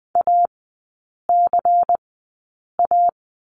Morse code, heard as a keyed tone, about 20 wpm, spaced as for 8 wpm, 700 Hz. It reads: ACA
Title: 093 Lesson 16e - Letter c - Three Characters 20wpm